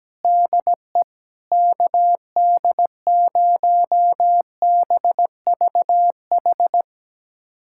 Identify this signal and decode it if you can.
Morse code, heard as a keyed tone, about 17 wpm, 700 Hz. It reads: DE KD0BVH
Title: USA Callsign DE Callsign - 6 Characters - 17wpm